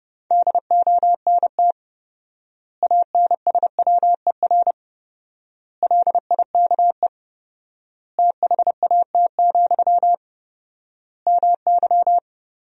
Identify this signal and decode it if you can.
Morse code, heard as a keyed tone, about 30 wpm, 700 Hz. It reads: DONT ANSWER LIKE THAT, MY
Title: 4x Word-Spacing - Sentences from Top 400 Words - 30wpm